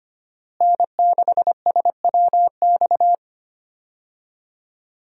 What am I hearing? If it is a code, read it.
Morse code, heard as a keyed tone, about 25 wpm, 700 Hz. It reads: N6SWX
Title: USA Call Signs - 5 Characters - Repeated 2x - 25wpm